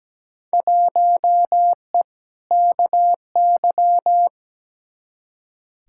Morse code, heard as a keyed tone, about 17 wpm, 700 Hz. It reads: 1E KY
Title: Field Day Exchanges - 17wpm